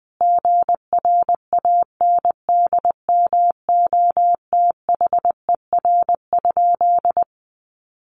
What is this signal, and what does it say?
Morse code, heard as a keyed tone, about 20 wpm, 700 Hz. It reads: GRANDMOTHER?
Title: Sentences from Top 1200 Words 20wpm